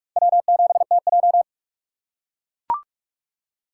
Morse code, heard as a keyed tone, about 44 wpm, 700 Hz. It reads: W7TJ
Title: USA Call Signs - 4 Characters - 4x Character Spacing - 45wpm